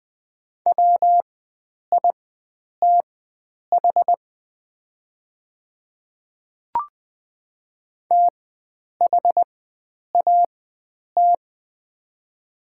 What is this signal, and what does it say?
Morse code, heard as a keyed tone, about 20 wpm, spaced as for 9 wpm, 690 Hz. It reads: WITH THAT
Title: 4x Character Spacing - Top 100 Words - 20wpm